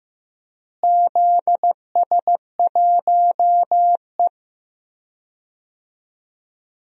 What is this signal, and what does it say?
Morse code, heard as a keyed tone, about 15 wpm, 700 Hz. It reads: ZS1E
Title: DXCC-100 Call Signs - 4 Characters - Repeated 3x - 15wpm